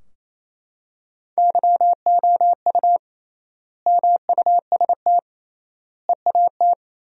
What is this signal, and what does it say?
Morse code, heard as a keyed tone, about 28 wpm, 700 Hz. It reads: YOU MUST EAT
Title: 3x Word-Spacing - Sentences from Top 300 Words - 28wpm